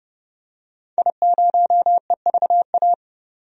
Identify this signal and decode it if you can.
Morse code, heard as a keyed tone, about 30 wpm, 700 Hz. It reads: I0EVA
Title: DXCC-100 Call Signs - 5 Characters - 30wpm